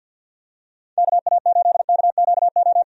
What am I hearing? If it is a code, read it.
Morse code, heard as a keyed tone, about 49 wpm, 700 Hz. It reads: KA8KQO
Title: USA Call Signs - 6 Characters - 50wpm